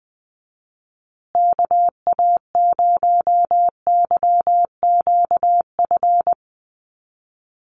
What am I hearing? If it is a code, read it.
Morse code, heard as a keyed tone, about 20 wpm, 700 Hz. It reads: KA0YQF
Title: USA Call Signs Repeated 2x 20wpm